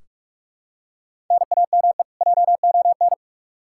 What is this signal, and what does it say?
Morse code, heard as a keyed tone, about 46 wpm, 700 Hz. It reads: NAME JON
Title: K1USN SST QSO Practice 45wpm